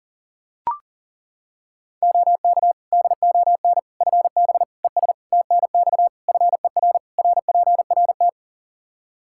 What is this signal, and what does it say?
Morse code, heard as a keyed tone, about 40 wpm, 700 Hz. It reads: OK DON FB ES TNX FER RPRT
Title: Practice QSO - Exchange 2 - Info - 40wpm